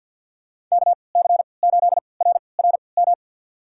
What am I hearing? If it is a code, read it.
Morse code, heard as a keyed tone, about 46 wpm, spaced as for 27 wpm, 700 Hz. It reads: KC8RRM
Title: USA Call Signs - 6 Characters - 3x Character Spacing - 50wpm